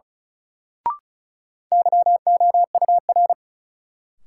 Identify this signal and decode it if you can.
Morse code, heard as a keyed tone, about 34 wpm, 700 Hz. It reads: YOUR
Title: Sentences from Top 100 Words - New Word Drill - 35wpm